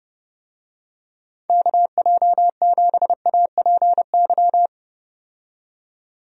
Morse code, heard as a keyed tone, about 30 wpm, 700 Hz. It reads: KJ7APY
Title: USA Call Signs - 6 Characters - 30wpm